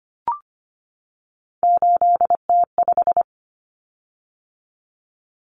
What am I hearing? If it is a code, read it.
Morse code, heard as a keyed tone, about 25 wpm, 700 Hz. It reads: 8T5
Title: Common Cut Numbers - 3-Digit 25wpm